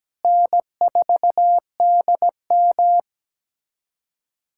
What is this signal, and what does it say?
Morse code, heard as a keyed tone, about 17 wpm, 700 Hz. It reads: N4DM
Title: USA Callsign DE Callsign - 4 Characters - 17wpm